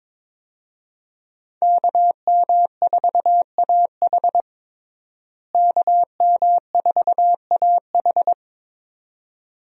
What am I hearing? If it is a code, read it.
Morse code, heard as a keyed tone, about 22 wpm, 700 Hz. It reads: KM4AH KM4AH
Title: USA Call Signs - 5 Characters - Repeated 2x - 22wpm